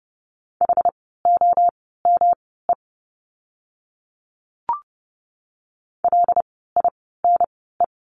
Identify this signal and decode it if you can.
Morse code, heard as a keyed tone, about 30 wpm, spaced as for 17 wpm, 700 Hz. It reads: HOME LINE
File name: course / 30:10wpm / 115 Lesson 19f - Letter w - Words 30wpm